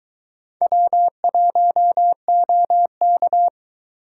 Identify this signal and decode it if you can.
Morse code, heard as a keyed tone, about 22 wpm, 700 Hz. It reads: W1OK